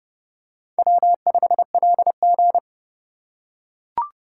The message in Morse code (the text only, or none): W5LG